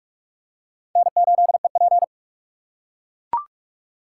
Morse code, heard as a keyed tone, about 42 wpm, 700 Hz. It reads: N8EP
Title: USA Call Signs - 4 Characters - 6x Character Spacing - 45wpm